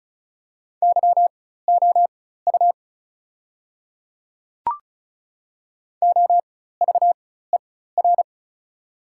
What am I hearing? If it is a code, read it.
Morse code, heard as a keyed tone, about 36 wpm, spaced as for 16 wpm, 690 Hz. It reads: YOU OVER
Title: 4x Character Spacing - Top 100 Words - 35wpm